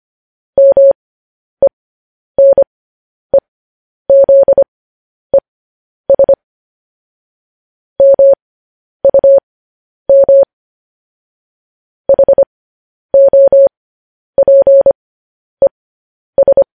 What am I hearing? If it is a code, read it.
Morse code, heard as a keyed tone, about 25 wpm, spaced as for 10 wpm, 550 Hz. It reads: MENEZES MUM HOPES
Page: news headlines in morse code at 10 wpm